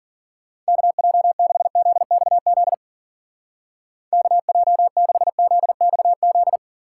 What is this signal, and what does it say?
Morse code, heard as a keyed tone, about 50 wpm, 700 Hz. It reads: KJ6ZXZ KJ6ZXZ
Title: Speed Racing - USA 2x3 Call Signs 40wpm